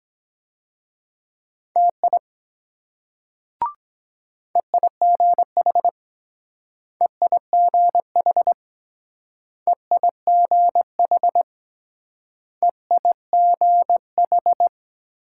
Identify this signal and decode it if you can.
Morse code, heard as a keyed tone, about 23 wpm, 700 Hz. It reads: TI EIGH EIGH EIGH EIGH